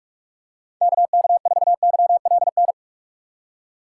Morse code, heard as a keyed tone, about 46 wpm, 700 Hz. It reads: KK4YLN
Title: USA Call Signs - 6 Characters - 4x Character Spacing - 45wpm